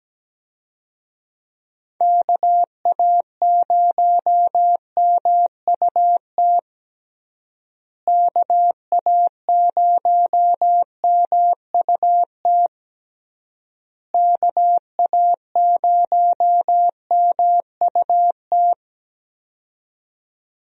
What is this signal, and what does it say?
Morse code, heard as a keyed tone, about 17 wpm, 700 Hz. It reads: KA0MUT KA0MUT KA0MUT
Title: USA Call Signs - 6 Characters - Repeated 3x - 17wpm